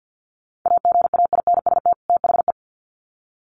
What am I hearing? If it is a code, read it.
Morse code, heard as a keyed tone, about 50 wpm, 700 Hz. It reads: AGAINST THE
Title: Sets of 2 Words 50wpm